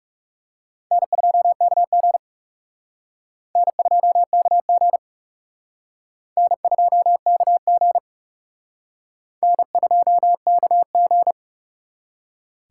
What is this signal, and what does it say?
Morse code, heard as a keyed tone, about 40 wpm, 700 Hz. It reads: N2KG N2KG N2KG N2KG